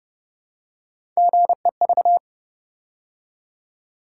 Morse code, heard as a keyed tone, about 29 wpm, 700 Hz. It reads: GEV